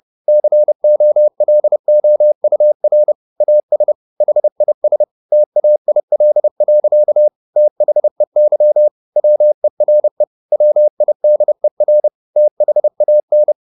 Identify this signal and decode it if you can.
Morse code, heard as a keyed tone, about 30 wpm, 600 Hz. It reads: COLOUR AS HIS TAIL. THEY WERE WIDER THAN